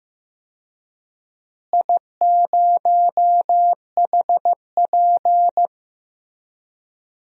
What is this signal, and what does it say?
Morse code, heard as a keyed tone, about 15 wpm, 700 Hz. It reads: I0HP
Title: DXCC-100 Call Signs - 4 Characters - Repeated 3x - 15wpm